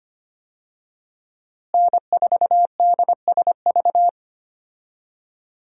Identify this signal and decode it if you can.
Morse code, heard as a keyed tone, about 25 wpm, 700 Hz. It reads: N4DSV
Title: USA Call Signs - 5 Characters - Repeated 2x - 25wpm